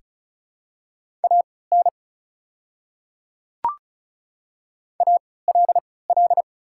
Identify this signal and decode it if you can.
Morse code, heard as a keyed tone, about 34 wpm, spaced as for 20 wpm, 700 Hz. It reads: AN ALL
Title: 3x Character Spacing - Top 100 Words - 35wpm